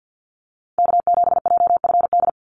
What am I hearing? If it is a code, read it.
Morse code, heard as a keyed tone, about 50 wpm, 700 Hz. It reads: K7JFD
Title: NA SOTA Activator Callsigns 50wpm